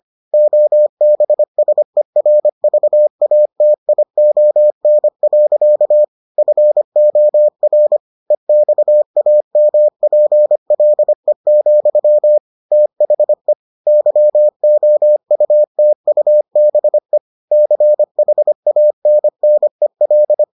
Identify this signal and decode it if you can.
Morse code, heard as a keyed tone, about 25 wpm, 600 Hz. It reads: OBSERVATION. FOR EXAMPLE, THE YOUTUBE CHANNEL